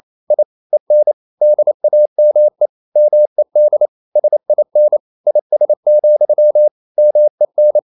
Morse code, heard as a keyed tone, about 28 wpm, 600 Hz. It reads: I EN DAME MED SIN IS, MEN
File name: Ferie QRQ 28wpm